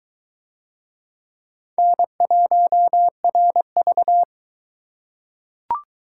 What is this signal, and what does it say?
Morse code, heard as a keyed tone, about 21 wpm, 700 Hz. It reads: N1RV